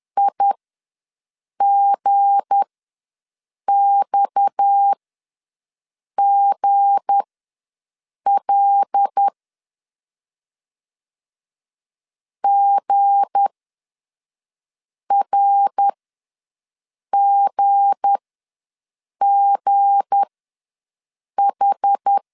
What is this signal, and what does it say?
Morse code, heard as a keyed tone, about 10 wpm, spaced as for 5 wpm, 800 Hz. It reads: IGXGL GRGGH
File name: LEZIONI / 7 GF9Z / 7 G ws 10wpm 3843